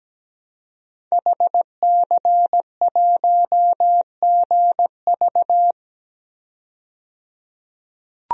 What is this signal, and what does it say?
Morse code, heard as a keyed tone, about 17 wpm, 700 Hz. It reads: HC1GV
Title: DXCC-100 Call Signs - 5 Characters - 17wpm